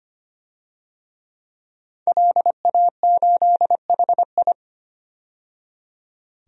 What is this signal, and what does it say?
Morse code, heard as a keyed tone, about 25 wpm, 700 Hz. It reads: LA8HI
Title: DXCC-100 Call Signs - 5 Characters - 25wpm